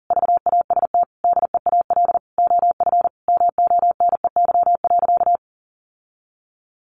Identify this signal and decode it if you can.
Morse code, heard as a keyed tone, about 37 wpm, 700 Hz. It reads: VAST DEAL OF MONEY.
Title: Sentences from Top 1300 Words 40wpm